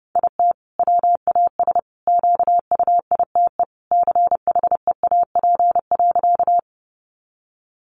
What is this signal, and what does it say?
Morse code, heard as a keyed tone, about 30 wpm, 700 Hz. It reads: IT WAS QUITE CHEAP.
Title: Sentences from Top 1200 Words 30wpm